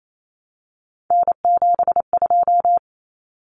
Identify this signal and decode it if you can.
Morse code, heard as a keyed tone, about 28 wpm, 700 Hz. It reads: N72
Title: Common Cut Numbers - 3-Digit 28wpm